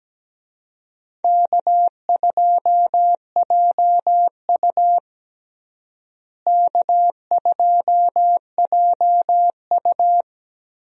Morse code, heard as a keyed tone, about 17 wpm, 700 Hz. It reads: K2JU K2JU
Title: USA Call Signs - 4 Characters - Repeated 2x - 17wpm